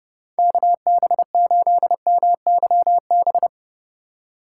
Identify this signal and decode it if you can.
Morse code, heard as a keyed tone, about 29 wpm, 700 Hz. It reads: KB8MYB